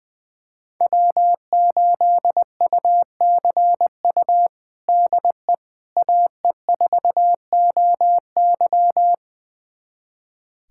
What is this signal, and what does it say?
Morse code, heard as a keyed tone, about 21 wpm, 700 Hz. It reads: W8UCU DE AE4OY
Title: USA Callsign DE Callsign 20wpm